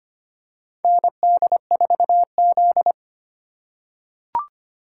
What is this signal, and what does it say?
Morse code, heard as a keyed tone, about 25 wpm, 700 Hz. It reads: ND4Z
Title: USA Call Signs - 4x Character Spacing - 25wpm